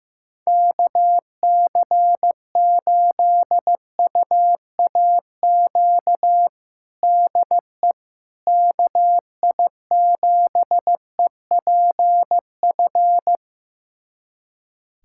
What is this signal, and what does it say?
Morse code, heard as a keyed tone, about 15 wpm, 700 Hz. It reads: KC8UAQ DE KI7EPF
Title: USA Callsign DE Callsign - 6 Characters - 15wpm